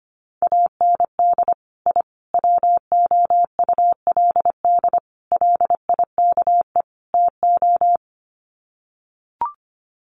Morse code, heard as a keyed tone, about 25 wpm, 700 Hz. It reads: AND I WOULD LIKE TO
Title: Sets of 5 Words Encore 25wpm